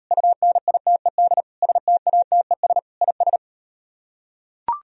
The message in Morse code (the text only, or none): UNITED STATES IS